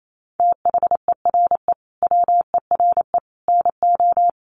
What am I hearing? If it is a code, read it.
Morse code, heard as a keyed tone, about 28 wpm, 700 Hz. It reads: THERE WERE NO
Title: Sets of 5 Words Encore2 28wpm